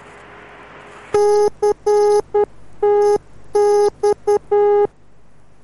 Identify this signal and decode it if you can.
Morse code, heard as a keyed tone, about 10 wpm, 410 Hz. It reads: CTX